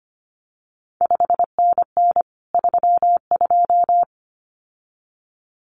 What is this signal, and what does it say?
Morse code, heard as a keyed tone, about 25 wpm, 700 Hz. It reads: 5NN 32
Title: CQ WW DX Contest Exchange Encore - 25wpm